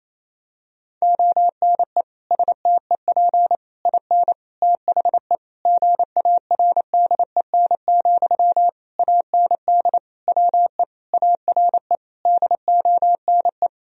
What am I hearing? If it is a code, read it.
Morse code, heard as a keyed tone, about 28 wpm, 700 Hz. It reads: ONE STEP IN THE GARDEN, AND WE ARE DONE
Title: Sentences from Top 1000 Words - Review Part 4 - 28wpm